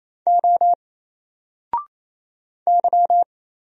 Morse code, heard as a keyed tone, about 27 wpm, 700 Hz. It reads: O Y
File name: Single-Letter-Number-28wpm